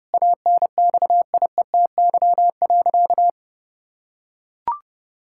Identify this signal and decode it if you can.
Morse code, heard as a keyed tone, about 30 wpm, 700 Hz. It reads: ANXIETY.
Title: Sentences from Top 1600 Words 30wpm